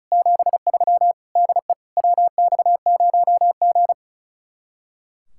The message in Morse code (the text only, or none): 73 DE WX0G